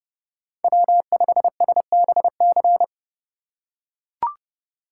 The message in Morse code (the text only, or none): W5SBC